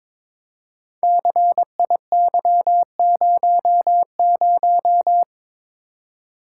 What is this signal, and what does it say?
Morse code, heard as a keyed tone, about 22 wpm, 700 Hz. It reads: CIY00
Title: Code Groups - 5 Letter:Numbers - 22wpm